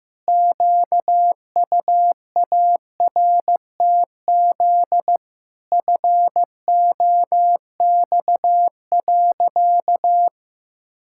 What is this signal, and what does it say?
Morse code, heard as a keyed tone, about 15 wpm, 700 Hz. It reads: QUARTZ FOX.
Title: Pangram Challenge v2 - 15wpm